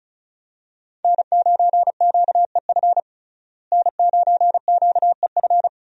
Morse code, heard as a keyed tone, about 34 wpm, 700 Hz. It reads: N9QEF N9QEF
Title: USA Call Signs - 5 Characters - Repeated 2x - 35wpm